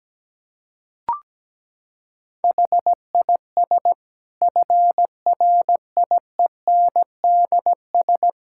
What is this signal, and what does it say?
Morse code, heard as a keyed tone, about 17 wpm, 700 Hz. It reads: HIS FRIENDS
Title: Sentences from Top 1600 Words 17wpm